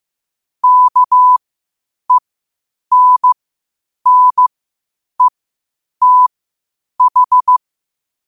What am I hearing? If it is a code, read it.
Morse code, heard as a keyed tone, about 15 wpm, spaced as for 8 wpm, 1000 Hz. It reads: KENNETH
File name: qotdfiles / 8wpm